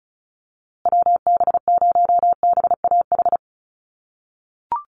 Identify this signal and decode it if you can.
Morse code, heard as a keyed tone, about 34 wpm, 700 Hz. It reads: WB0BAH
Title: USA Call Signs Repeated 3x 35wpm